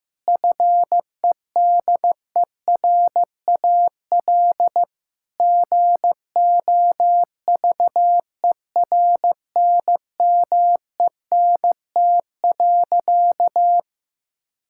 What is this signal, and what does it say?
Morse code, heard as a keyed tone, about 16 wpm, 700 Hz. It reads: FEDERAL GOVERNMENT.
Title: Sentences from Top 1000 Words - Review Part 5 - 15wpm